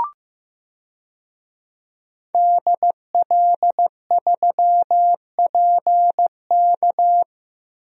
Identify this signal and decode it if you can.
Morse code, heard as a keyed tone, about 14 wpm, 700 Hz. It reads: DL3PK
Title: DXCC-100 Call Signs - 5 Characters - Repeated 2x - 15wpm